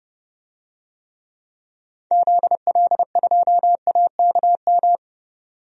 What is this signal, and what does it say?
Morse code, heard as a keyed tone, about 30 wpm, 690 Hz. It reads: ZL2AKM
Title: DXCC-100 Call Signs - 6 Characters - 30wpm